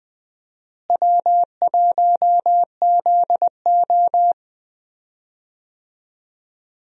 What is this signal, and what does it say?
Morse code, heard as a keyed tone, about 20 wpm, 700 Hz. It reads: W1ZO
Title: USA Call Signs - 4 Characters - Repeated 2x - 20wpm